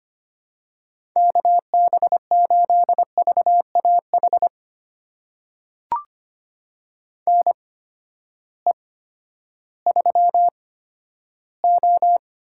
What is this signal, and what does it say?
Morse code, heard as a keyed tone, about 25 wpm, spaced as for 12 wpm, 700 Hz. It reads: KB8VAH NE3O
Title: USA Call Signs - 8x Character Spacing - 25wpm